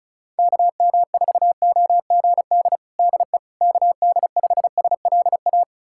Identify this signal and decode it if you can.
Morse code, heard as a keyed tone, about 34 wpm, 700 Hz. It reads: KM4OGD DE KD5SLA
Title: USA Callsign DE Callsign - 6 Characters - 35wpm